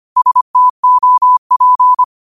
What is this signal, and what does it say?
Morse code, heard as a keyed tone, about 25 wpm, 1000 Hz. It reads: STOP
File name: qotdfiles / 25wpm